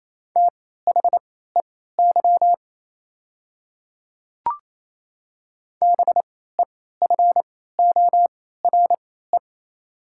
Morse code, heard as a keyed tone, about 28 wpm, spaced as for 16 wpm, 700 Hz. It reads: THEY BEFORE
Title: 3x Character Spacing - Top 200 Words - 28wpm